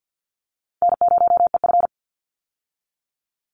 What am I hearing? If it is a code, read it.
Morse code, heard as a keyed tone, about 48 wpm, 700 Hz. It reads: N0EF
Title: Call Signs Repeated 2x - 50wpm